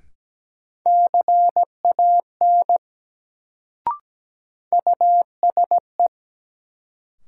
Morse code, heard as a keyed tone, about 17 wpm, 700 Hz. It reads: CAN USE
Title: Sentences from Top 100 Words - New Word Drill - 17wpm